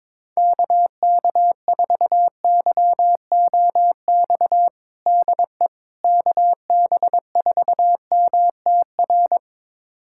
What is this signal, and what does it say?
Morse code, heard as a keyed tone, about 22 wpm, 700 Hz. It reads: KK4YOX DE KB4MTR
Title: USA Callsign DE Callsign 22wpm